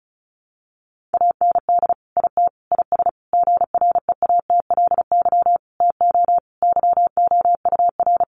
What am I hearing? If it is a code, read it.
Morse code, heard as a keyed tone, about 34 wpm, 700 Hz. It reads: AND IT IS GREATLY TO YOUR
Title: Sentences from Top 1400 Words 35wpm